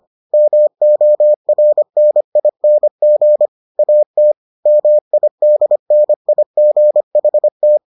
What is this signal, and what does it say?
Morse code, heard as a keyed tone, about 25 wpm, 600 Hz. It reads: MORNING AT MIDNIGHT